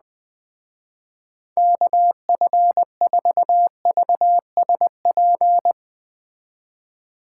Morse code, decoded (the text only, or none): KF4VSP